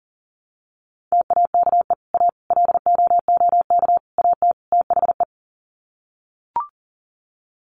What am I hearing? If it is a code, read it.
Morse code, heard as a keyed tone, about 40 wpm, 700 Hz. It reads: TAKE A LOOK AT THE
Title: Sets of 5 Words 40wpm